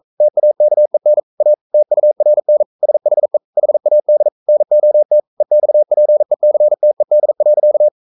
Morse code, heard as a keyed tone, about 42 wpm, 600 Hz. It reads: TAKEN A TURN SHE HAD NOT EXPECTED.
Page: Romanen Never af Ken Follet på engelsk.